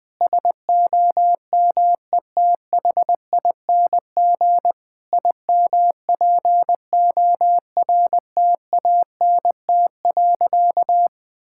Morse code, heard as a keyed tone, about 21 wpm, 700 Hz. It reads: SOMETHING IMPORTANT.
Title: Sentences with 5-Word Phrases - 20wpm